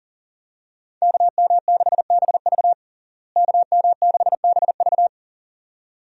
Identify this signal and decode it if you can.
Morse code, heard as a keyed tone, about 40 wpm, 700 Hz. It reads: KM6BV KM6BV
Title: USA Call Signs - 5 Characters - Repeated 2x - 40wpm